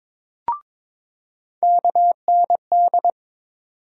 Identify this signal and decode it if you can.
Morse code, heard as a keyed tone, about 22 wpm, 700 Hz. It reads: KND
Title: USA Call Sign Postfixes - 3 Letters 22wpm